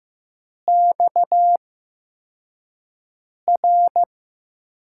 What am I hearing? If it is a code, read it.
Morse code, heard as a keyed tone, about 15 wpm, spaced as for 4 wpm, 700 Hz. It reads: XR
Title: USA Call Signs - 4 Characters - 8x Character Spacing - 15wpm